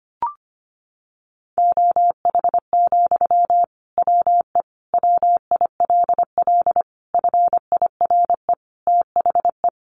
Morse code, heard as a keyed tone, about 25 wpm, 700 Hz. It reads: OH, WE WILL FIRE THE
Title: Sentences from Top 1100 Words 25wpm